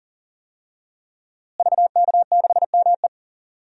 Morse code, heard as a keyed tone, about 40 wpm, 700 Hz. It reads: VK6ME
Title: DXCC-100 Call Signs - 5 Characters - Repeated 2x - 40wpm